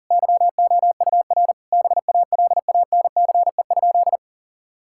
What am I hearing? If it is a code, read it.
Morse code, heard as a keyed tone, about 40 wpm, 700 Hz. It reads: YOUR BALANCE?
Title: Sentences from Top 1000 Words - Review Part 6 - 40wpm